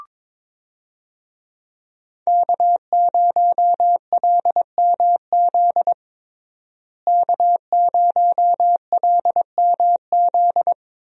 Morse code, heard as a keyed tone, about 22 wpm, 700 Hz. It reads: K0LMZ K0LMZ
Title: USA Call Signs - 5 Characters - Repeated 3x - 22wpm